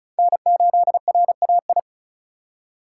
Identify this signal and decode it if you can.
Morse code, heard as a keyed tone, about 36 wpm, 700 Hz. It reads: N8RAI
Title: USA Call Signs - 5 Characters - 3x Character Spacing - 35wpm